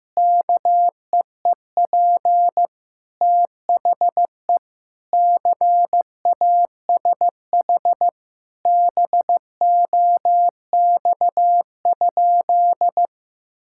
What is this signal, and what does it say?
Morse code, heard as a keyed tone, about 15 wpm, 700 Hz. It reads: KEEP THE CASH BOX?
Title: Sentences from Top 1000 Words - Review Part 5 - 15wpm